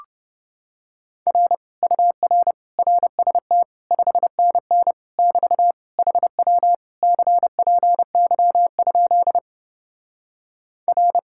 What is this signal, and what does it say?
Morse code, heard as a keyed tone, about 29 wpm, 700 Hz. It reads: R UR RST 5NN <BT> HW CPY? R